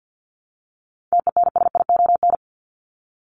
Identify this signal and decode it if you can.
Morse code, heard as a keyed tone, about 50 wpm, 700 Hz. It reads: TENSION
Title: Top 1501-1750 Words 50wpm